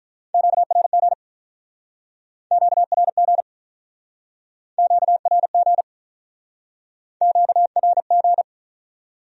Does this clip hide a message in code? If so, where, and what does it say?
Yes: Morse code from 0.3 to 8.4 s, heard as a keyed tone, about 47 wpm, 700 Hz. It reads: QRG QRG QRG QRG